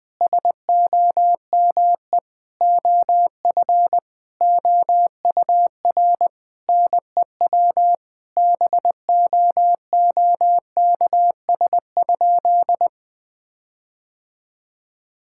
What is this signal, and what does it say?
Morse code, heard as a keyed tone, about 20 wpm, 700 Hz. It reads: SOME OF OUR NEW BOOKS?
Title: Sentences with 9-Word Phrases - 20wpm